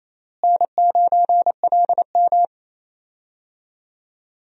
Morse code, heard as a keyed tone, about 28 wpm, 700 Hz. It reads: N9LM
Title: USA Call Signs - 4 Characters - Repeated 3x - 28wpm